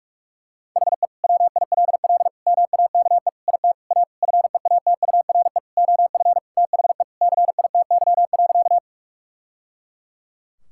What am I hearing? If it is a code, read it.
Morse code, heard as a keyed tone, about 46 wpm, 700 Hz. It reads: HE WILL MAKE IT A FEATURE OF THE CITY.
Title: Sentences from Top 1000 Words - Review Part 5 - 45wpm